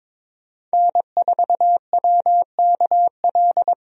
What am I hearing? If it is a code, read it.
Morse code, heard as a keyed tone, about 22 wpm, 700 Hz. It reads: N4WKL
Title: USA Call Signs - 5 Characters - Repeated 3x - 22wpm